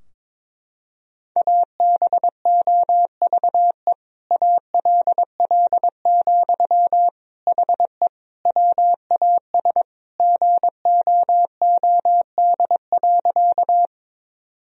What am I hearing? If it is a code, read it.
Morse code, heard as a keyed tone, about 22 wpm, 700 Hz. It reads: ABOVE ALL, HE WAS GOOD.
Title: Sentences from Top 1000 Words - Review Part 2 - 22wpm